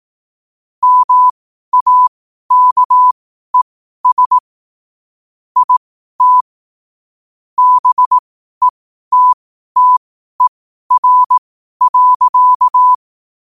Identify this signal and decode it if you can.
Morse code, heard as a keyed tone, about 18 wpm, spaced as for 12 wpm, 1000 Hz. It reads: MAKES IT BETTER.
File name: qotdfiles / 12wpm